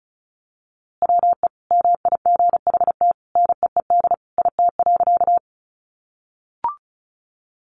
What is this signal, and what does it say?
Morse code, heard as a keyed tone, about 34 wpm, 700 Hz. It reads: WE MIGHT NEED IT.
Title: A Sound Of Thunder 35wpm